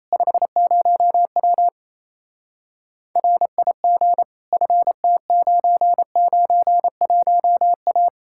Morse code, heard as a keyed tone, about 36 wpm, 700 Hz. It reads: 50W RIG FT991A